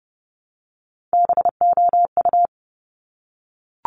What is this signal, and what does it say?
Morse code, heard as a keyed tone, about 30 wpm, 700 Hz. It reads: BOU
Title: Top Three Letter Combinations in Words 30wpm